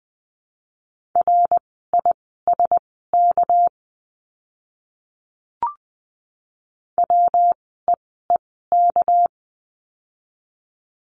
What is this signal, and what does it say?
Morse code, heard as a keyed tone, about 20 wpm, spaced as for 15 wpm, 700 Hz. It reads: RISK WEEK
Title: 199 Lesson 32f - Letter k - Words 20wpm